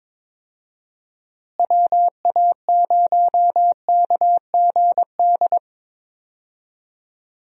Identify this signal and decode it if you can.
Morse code, heard as a keyed tone, about 22 wpm, 700 Hz. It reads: WA0KGD
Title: USA Call Signs - 6 Characters - Repeated 2x - 22wpm